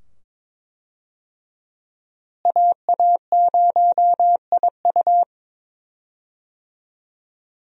Morse code, heard as a keyed tone, about 22 wpm, 700 Hz. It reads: AA0IU
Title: USA Call Signs - 5 Characters - Repeated 2x - 22wpm